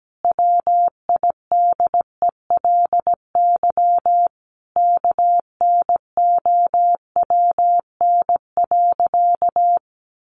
Morse code, heard as a keyed tone, about 17 wpm, 700 Hz. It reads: WIDELY KNOWN.
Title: Sentences from Top 1200 Words 17wpm